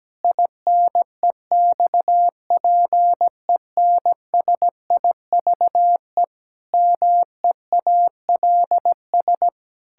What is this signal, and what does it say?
Morse code, heard as a keyed tone, about 17 wpm, 700 Hz. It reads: INEXPENSIVE MEALS
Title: Pangram Challenge v2 - 17wpm